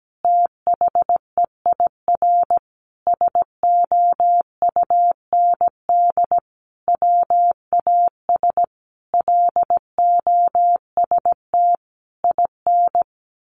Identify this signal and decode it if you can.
Morse code, heard as a keyed tone, about 17 wpm, 700 Hz. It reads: THEIR SOUND WAS LOST IN
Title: A Sound Of Thunder 17wpm